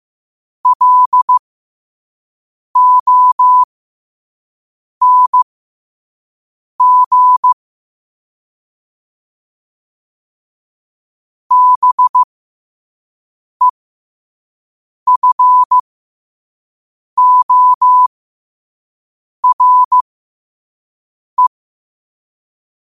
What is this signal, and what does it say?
Morse code, heard as a keyed tone, about 15 wpm, spaced as for 5 wpm, 1000 Hz. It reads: LONG BEFORE
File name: qotdfiles / 5wpm